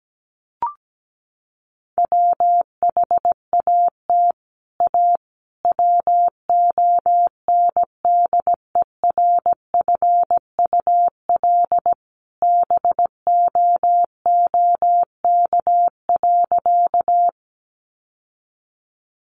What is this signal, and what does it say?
Morse code, heard as a keyed tone, about 17 wpm, 700 Hz. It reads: WHAT A WONDERFUL BOOK.
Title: Sentences from Top 1100 Words 17wpm